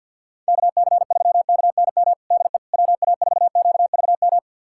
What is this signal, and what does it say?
Morse code, heard as a keyed tone, about 52 wpm, 700 Hz. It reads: KC3KNM DE WA4QVM
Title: USA Callsign DE Callsign - 6 Characters - 50wpm